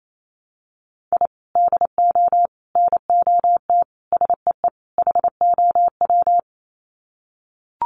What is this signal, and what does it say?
Morse code, heard as a keyed tone, about 28 wpm, 700 Hz. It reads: I DO NOT SEE HOW
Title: Sets of 5 Words 28wpm